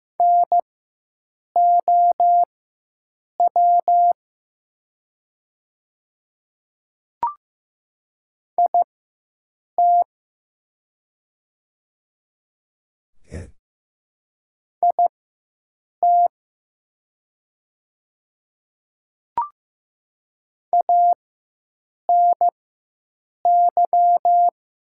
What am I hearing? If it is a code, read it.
Morse code, heard as a keyed tone, about 15 wpm, spaced as for 7 wpm, 700 Hz. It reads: NOW IT IT ANY
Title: 4x Character Spacing - Top 300 Words - 15wpm